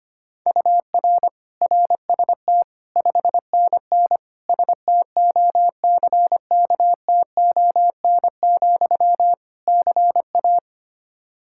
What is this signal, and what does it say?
Morse code, heard as a keyed tone, about 25 wpm, 700 Hz. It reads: UR RST 5NN STOCKTON, CA
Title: K1USN SST QSO Practice 25wpm